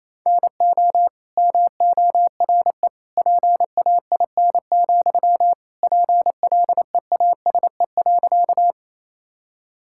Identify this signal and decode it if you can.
Morse code, heard as a keyed tone, about 28 wpm, 700 Hz. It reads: NO MORE PAIN, PLEASE.
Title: Sentences from Top 1600 Words 28wpm